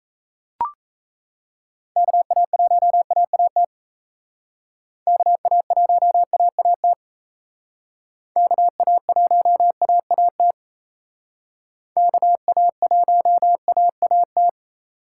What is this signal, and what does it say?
Morse code, heard as a keyed tone, about 36 wpm, 700 Hz. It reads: KA1AAT KA1AAT KA1AAT KA1AAT